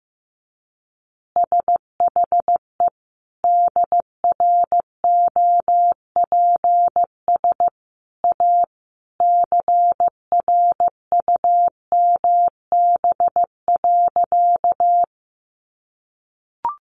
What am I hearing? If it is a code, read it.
Morse code, heard as a keyed tone, about 15 wpm, 700 Hz. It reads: SHE DROPS A CRUMB.
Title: Additional Easy Phrases 15wpm